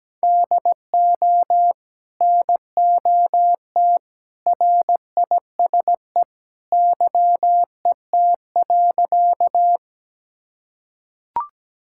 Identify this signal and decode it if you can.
Morse code, heard as a keyed tone, about 17 wpm, 700 Hz. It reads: DO NOT RISE YET.
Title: Sentences from Top 1000 Words - Review Part 6 - 17wpm